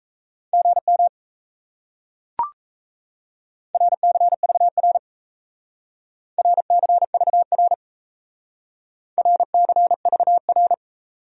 Morse code, heard as a keyed tone, about 37 wpm, 700 Hz. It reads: GM RCVR RCVR RCVR